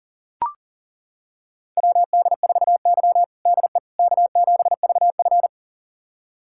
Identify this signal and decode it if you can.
Morse code, heard as a keyed tone, about 40 wpm, 700 Hz. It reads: WD4Y DE K7VF
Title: USA Callsign DE Callsign - 4 Characters - 40wpm